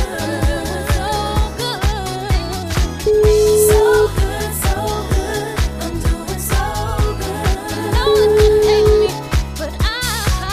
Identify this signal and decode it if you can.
Ringback tone (Europe, 425 Hz), with noise.